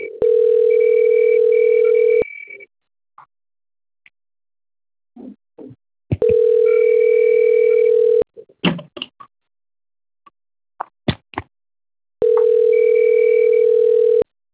Ringback tone (North America), 440 + 480 Hz.